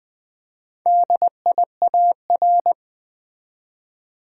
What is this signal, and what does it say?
Morse code, heard as a keyed tone, about 20 wpm, 700 Hz. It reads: DIAR
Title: Build A Word - 1 Letter At A Time - 5-Letter Words - 20wpm